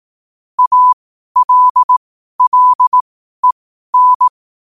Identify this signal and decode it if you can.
Morse code, heard as a keyed tone, about 18 wpm, spaced as for 12 wpm, 1000 Hz. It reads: ALLEN
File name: qotdfiles / 12wpm